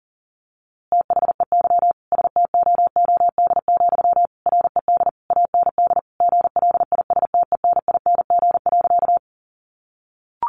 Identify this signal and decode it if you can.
Morse code, heard as a keyed tone, about 37 wpm, 700 Hz. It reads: THEY STOOD, RED AND GLISTENING.
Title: A Sound Of Thunder 40wpm